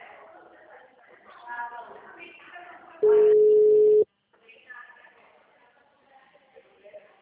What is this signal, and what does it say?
Ringback tone (Europe, 425 Hz), with speech.